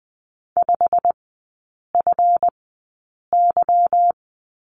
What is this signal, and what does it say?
Morse code, heard as a keyed tone, about 20 wpm, spaced as for 8 wpm, 700 Hz. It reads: 5FY
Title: 159 Lesson 26e - Letter g - Three Characters 20wpm